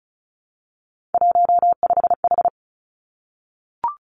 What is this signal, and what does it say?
Morse code, heard as a keyed tone, about 36 wpm, 700 Hz. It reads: 15H
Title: Three Challenging Letter-Number Combinations 35wpm